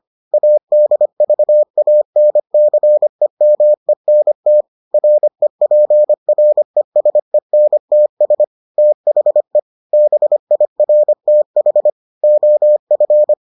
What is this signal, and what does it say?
Morse code, heard as a keyed tone, about 25 wpm, 600 Hz. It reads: ADVANCEMENT REPRESENTS THE BIRTH OF